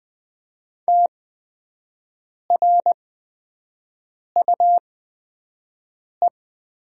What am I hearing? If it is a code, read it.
Morse code, heard as a keyed tone, about 20 wpm, spaced as for 5 wpm, 700 Hz. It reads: TRUE
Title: 8x Character Spacing - Top 500 Words - 20wpm